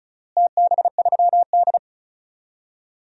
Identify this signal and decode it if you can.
Morse code, heard as a keyed tone, about 36 wpm, 700 Hz. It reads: TB3D
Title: DXCC-100 Call Signs - 4 Characters - Repeated 2x - 35wpm